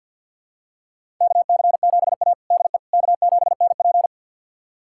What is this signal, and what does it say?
Morse code, heard as a keyed tone, about 51 wpm, 690 Hz. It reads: KC7A DE K7NP
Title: USA Callsign DE Callsign - 4 Characters - 50wpm